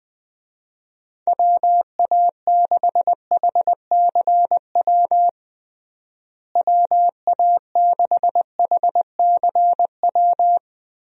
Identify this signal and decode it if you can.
Morse code, heard as a keyed tone, about 20 wpm, 700 Hz. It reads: WA6HCW WA6HCW
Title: USA Call Signs - 6 Characters - Repeated 2x - 20wpm